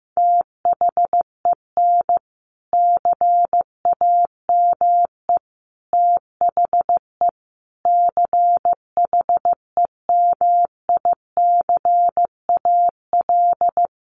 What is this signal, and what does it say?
Morse code, heard as a keyed tone, about 15 wpm, 700 Hz. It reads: THEN CAME THE CHEMICAL
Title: Sentences from Top 1200 Words 15wpm